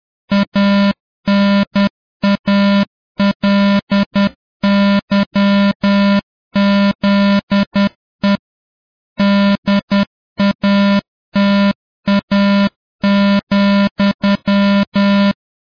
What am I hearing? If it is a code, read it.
Morse code, heard as a keyed tone, about 10 wpm, 200 Hz. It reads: ANALYZE DATA,